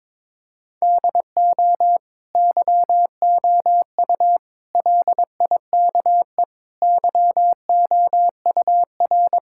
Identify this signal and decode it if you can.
Morse code, heard as a keyed tone, about 21 wpm, 700 Hz. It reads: DO YOU LIKE YOUR
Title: Sentences from Top 1000 Words - Review Part 6 - 22wpm